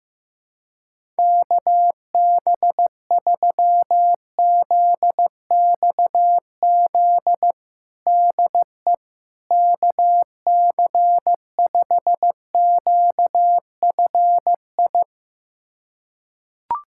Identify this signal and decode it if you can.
Morse code, heard as a keyed tone, about 15 wpm, 700 Hz. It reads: KB3ZXZ DE KC5QFI
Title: USA Callsign DE Callsign - 6 Characters - 15wpm